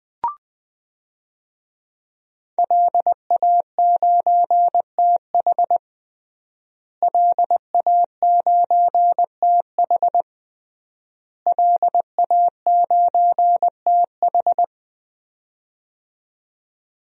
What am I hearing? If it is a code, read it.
Morse code, heard as a keyed tone, about 20 wpm, 700 Hz. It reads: LA9TH LA9TH LA9TH
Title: DXCC-100 Call Signs - 5 Characters - Repeated 3x - 20wpm